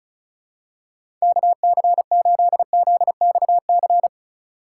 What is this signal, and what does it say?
Morse code, heard as a keyed tone, about 36 wpm, 700 Hz. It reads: KC8ZXC
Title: USA Call Signs - 6 Characters - 8x Character Spacing - 35wpm